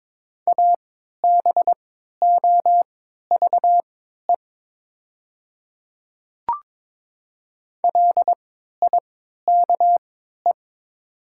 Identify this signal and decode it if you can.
Morse code, heard as a keyed tone, about 22 wpm, spaced as for 12 wpm, 700 Hz. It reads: ABOVE LIKE
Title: 3x Character Spacing - Top 500 Words - 22wpm